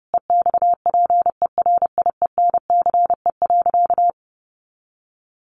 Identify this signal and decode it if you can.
Morse code, heard as a keyed tone, about 30 wpm, 700 Hz. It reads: EXPERIENCE.
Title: Sentences from Top 1300 Words 30wpm